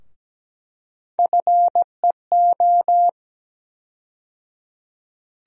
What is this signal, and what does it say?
Morse code, heard as a keyed tone, about 17 wpm, 700 Hz. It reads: FEO
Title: USA Call Sign Postfixes - 3 Letters 17wpm